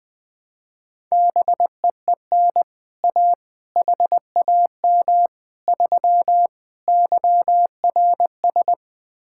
Morse code, heard as a keyed tone, about 20 wpm, 700 Hz. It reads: BEEN A HAM 3 YRS
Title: Speed-Racing Common QSO Phrases - 20wpm